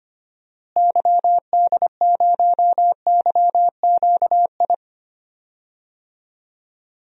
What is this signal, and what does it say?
Morse code, heard as a keyed tone, about 25 wpm, 700 Hz. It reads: YD0YQI
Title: DXCC-100 Call Signs - 6 Characters - Repeated 2x - 25wpm